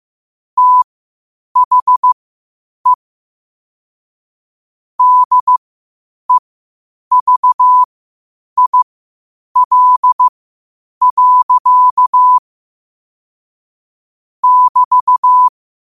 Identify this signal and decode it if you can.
Morse code, heard as a keyed tone, about 15 wpm, spaced as for 8 wpm, 1000 Hz. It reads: THE DEVIL. <BT>
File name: qotdfiles / 8wpm